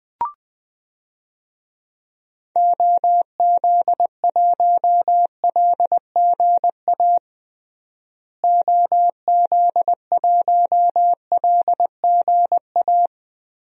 Morse code, heard as a keyed tone, about 20 wpm, 700 Hz. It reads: OZ1LGA OZ1LGA
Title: DXCC-100 Call Signs - 6 Characters - Repeated 2x - 20wpm